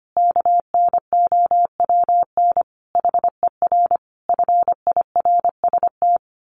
Morse code, heard as a keyed tone, about 25 wpm, 700 Hz. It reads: KNOWN HER FIRST
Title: Sentences from Top 1200 Words 25wpm